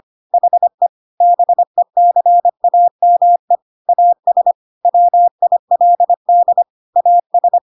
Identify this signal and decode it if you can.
Morse code, heard as a keyed tone, about 25 wpm, 700 Hz. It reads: HE BECAME AS WILD AS
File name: Pinocchio / Pinocchio ch2 25 25 ik1hge